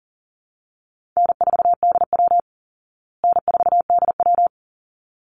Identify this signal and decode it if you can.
Morse code, heard as a keyed tone, about 40 wpm, 700 Hz. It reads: N4DW N4DW
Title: Easy Call Signs Repeated 2x - 40wpm